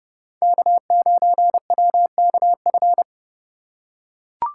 Morse code, heard as a keyed tone, about 30 wpm, 700 Hz. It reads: K9WKF
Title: USA Call Signs - 5 Characters - 4x Character Spacing - 30wpm